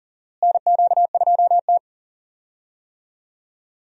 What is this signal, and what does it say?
Morse code, heard as a keyed tone, about 40 wpm, 700 Hz. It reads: NQ2T
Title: USA Call Signs - 4 Characters - Repeated 3x - 40wpm